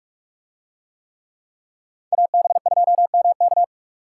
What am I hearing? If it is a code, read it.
Morse code, heard as a keyed tone, about 44 wpm, 700 Hz. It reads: AB2MK